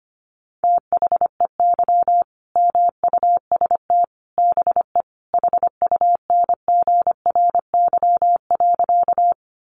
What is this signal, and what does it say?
Morse code, heard as a keyed tone, about 25 wpm, 700 Hz. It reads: THEY MUST BE HUNGRY.
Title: Sentences from Top 1400 Words 25wpm